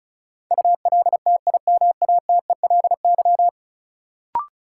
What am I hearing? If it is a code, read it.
Morse code, heard as a keyed tone, about 34 wpm, 700 Hz. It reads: ULTIMATELY
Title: Sentences from Top 1200 Words - New Word Drill - 35wpm